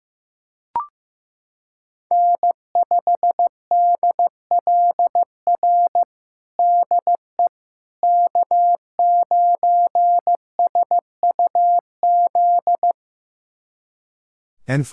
Morse code, heard as a keyed tone, about 15 wpm, 700 Hz. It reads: N5DLR DE K9SUZ
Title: USA Callsign DE Callsign - 5 Characters - 15wpm